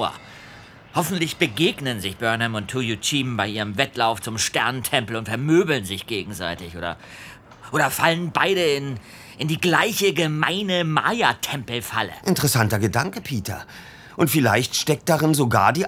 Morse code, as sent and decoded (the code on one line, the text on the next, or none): none